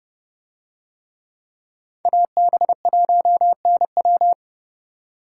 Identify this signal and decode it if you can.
Morse code, heard as a keyed tone, about 30 wpm, 700 Hz. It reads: AB1NW